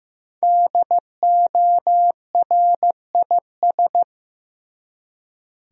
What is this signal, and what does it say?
Morse code, heard as a keyed tone, about 15 wpm, 700 Hz. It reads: DORIS
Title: USA Common Names - Max 5 Letters - 15wpm